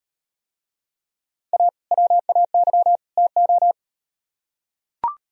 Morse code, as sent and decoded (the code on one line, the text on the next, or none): .- / .-- .- -.-- / - ---
A WAY TO